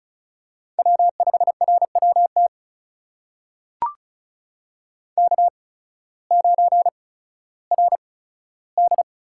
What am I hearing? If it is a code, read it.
Morse code, heard as a keyed tone, about 36 wpm, spaced as for 16 wpm, 700 Hz. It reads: W5RWT K9RD
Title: USA Call Signs - 5 Characters - 8x Character Spacing - 35wpm